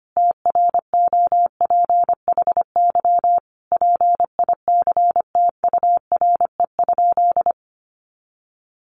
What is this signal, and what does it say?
Morse code, heard as a keyed tone, about 25 wpm, 700 Hz. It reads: TROPHY PICTURE?
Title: A Sound Of Thunder 25wpm